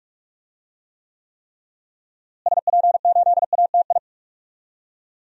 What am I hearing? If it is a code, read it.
Morse code, heard as a keyed tone, about 46 wpm, 700 Hz. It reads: SP8ATI
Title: DXCC-100 Call Signs - 6 Characters - Repeated 3x - 45wpm